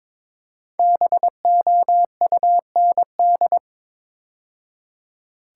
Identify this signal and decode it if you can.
Morse code, heard as a keyed tone, about 22 wpm, 690 Hz. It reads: BOUND
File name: 5 Letter Words - First 3 Letters Given - 22wpm